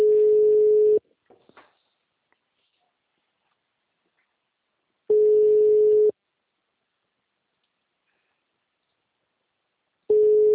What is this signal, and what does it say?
Ringback tone (Europe, 425 Hz).